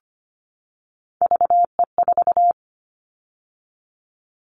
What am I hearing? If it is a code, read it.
Morse code, heard as a keyed tone, about 25 wpm, 700 Hz. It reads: VE4
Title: Callsign Prefixes for Top 50 DXCC - 25wpm